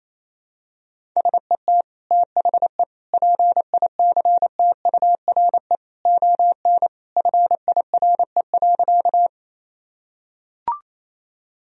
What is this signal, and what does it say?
Morse code, heard as a keyed tone, about 28 wpm, 700 Hz. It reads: SET THE PICTURE ON FIRE.
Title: Sentences from Top 1000 Words - Review Part 1 - 28wpm